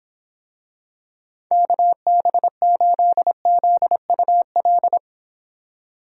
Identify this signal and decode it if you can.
Morse code, heard as a keyed tone, about 26 wpm, 700 Hz. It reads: KB8ZUL